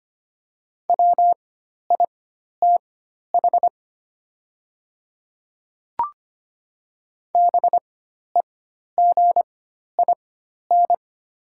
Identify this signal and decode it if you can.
Morse code, heard as a keyed tone, about 25 wpm, spaced as for 12 wpm, 700 Hz. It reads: WITH BEGIN
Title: 4x Character Spacing - Top 300 Words - 25wpm